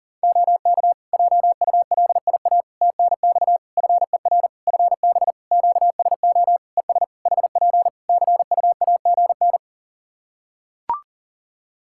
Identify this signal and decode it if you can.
Morse code, heard as a keyed tone, about 40 wpm, 700 Hz. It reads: OK JULIA TNX FER FB QSO ES HP CUAGN
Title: Practice QSO - Exchange 4 - Ending - 40wpm